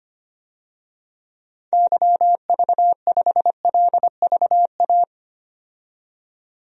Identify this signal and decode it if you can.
Morse code, heard as a keyed tone, about 25 wpm, 700 Hz. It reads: YV5LVA
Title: DXCC-100 Call Signs - 6 Characters - 25wpm